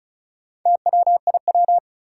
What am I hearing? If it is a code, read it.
Morse code, heard as a keyed tone, about 36 wpm, 700 Hz. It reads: TWIW
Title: Four Challenging Letter-Number Combinations 35wpm